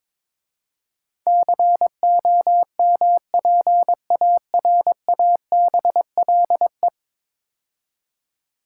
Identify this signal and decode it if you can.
Morse code, heard as a keyed tone, about 21 wpm, 700 Hz. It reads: COMPARABLE
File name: practice / 22wpm / 10 Letter Words - First 3 Letters Given - 22wpm